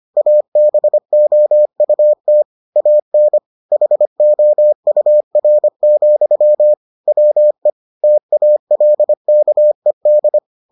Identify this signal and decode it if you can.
Morse code, heard as a keyed tone, about 25 wpm, 600 Hz. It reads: ABOUT AN HOUR, WE TALKED